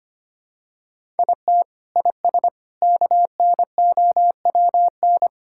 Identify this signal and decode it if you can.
Morse code, heard as a keyed tone, about 25 wpm, 700 Hz. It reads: IT IS KNOWN
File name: practice / 25wpm / Sentences with 4-Word Phrases - 25wpm